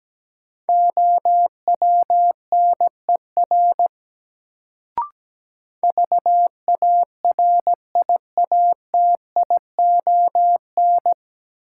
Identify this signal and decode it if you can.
Morse code, heard as a keyed tone, about 17 wpm, 700 Hz. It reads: OWNER VARIATION
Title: Sentences from Top 1600 Words - New Word Drill Alt - 17wpm